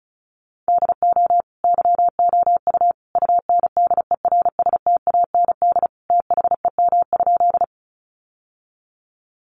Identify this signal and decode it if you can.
Morse code, heard as a keyed tone, about 34 wpm, 700 Hz. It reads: DO YOU UNDERSTAND THEM?
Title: Sentences from Top 600 Words 35wpm